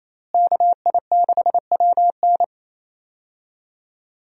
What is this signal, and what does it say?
Morse code, heard as a keyed tone, about 28 wpm, 700 Hz. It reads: KI6WN
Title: USA Call Signs - 5 Characters - Repeated 2x - 28wpm